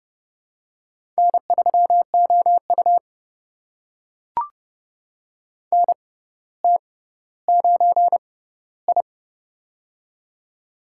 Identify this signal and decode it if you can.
Morse code, heard as a keyed tone, about 30 wpm, spaced as for 15 wpm, 700 Hz. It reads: N3OU NT9I
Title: USA Call Signs - 4 Characters - 6x Character Spacing - 30wpm